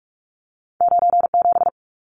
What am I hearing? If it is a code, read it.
Morse code, heard as a keyed tone, about 46 wpm, 700 Hz. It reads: 97
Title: Numbers - 2-Digit 45wpm